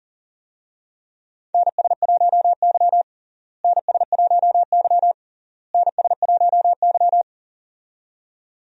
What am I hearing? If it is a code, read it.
Morse code, heard as a keyed tone, about 40 wpm, 700 Hz. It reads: NS1Y NS1Y NS1Y
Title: USA Call Signs - 4 Characters - Repeated 3x - 40wpm